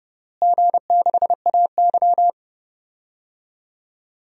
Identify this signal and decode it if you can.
Morse code, heard as a keyed tone, about 30 wpm, 700 Hz. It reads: G6AY
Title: DXCC-100 Call Signs - 4 Characters - Repeated 3x - 30wpm